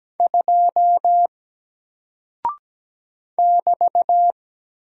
Morse code, heard as a keyed tone, about 17 wpm, 700 Hz. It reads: 2 <BT>
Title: Single Character (Letter:Number:Punctuation:Prosign) 17wpm